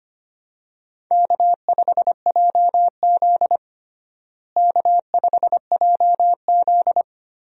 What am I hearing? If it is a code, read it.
Morse code, heard as a keyed tone, about 25 wpm, 700 Hz. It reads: K5JZ K5JZ
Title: USA Call Signs - 4 Characters - Repeated 2x - 25wpm